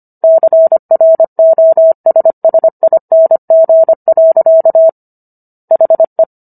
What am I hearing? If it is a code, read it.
Morse code, heard as a keyed tone, about 25 wpm, 650 Hz. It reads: CROSSING. HE